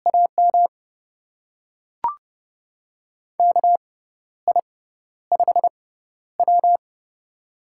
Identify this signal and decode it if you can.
Morse code, heard as a keyed tone, about 30 wpm, spaced as for 12 wpm, 700 Hz. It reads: AM KI5W
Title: USA Call Signs - 6x Character Spacing - 30wpm